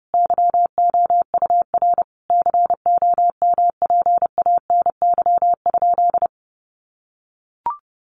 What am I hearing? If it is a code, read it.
Morse code, heard as a keyed tone, about 30 wpm, 700 Hz. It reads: YOUR COMPANY?
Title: Sentences from Top 1300 Words 30wpm